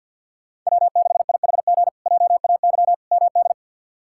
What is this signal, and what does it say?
Morse code, heard as a keyed tone, about 50 wpm, 700 Hz. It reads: W6IHG JAY MD
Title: K1USN SST QSO Practice - 50wpm